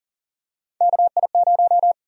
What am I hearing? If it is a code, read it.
Morse code, heard as a keyed tone, about 38 wpm, 700 Hz. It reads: KI0
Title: USA Call Sign Prefixes - 40wpm